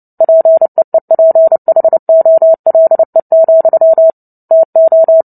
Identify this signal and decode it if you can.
Morse code, heard as a keyed tone, about 30 wpm, 650 Hz. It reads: PEEPHOLE, TO
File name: Book Two / 03-Chapter 3 - 29 WPM